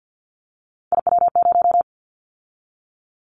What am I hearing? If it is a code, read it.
Morse code, heard as a keyed tone, about 50 wpm, 700 Hz. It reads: IW0
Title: Callsign Prefixes for Top 25 DXCC - 50wpm